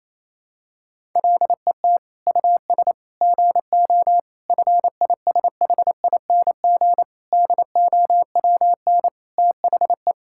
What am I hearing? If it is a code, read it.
Morse code, heard as a keyed tone, about 28 wpm, 700 Hz. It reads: LET US GO FISHING DOWN THE
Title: Sentences from Top 1000 Words - Review Part 5 - 28wpm